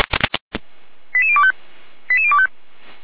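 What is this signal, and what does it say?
Five-tone selective call (ZVEI), heard as quick runs of five tones, tones of about 70 ms. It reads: 88025 88025
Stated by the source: ZVEI five-tone selective call received on an SDR, fire brigade.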